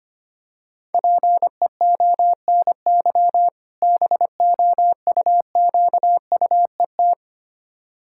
Morse code, heard as a keyed tone, about 25 wpm, 700 Hz. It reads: PEONY BOUQUET
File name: practice / 25wpm / Difficult Words with Hints - 25wpm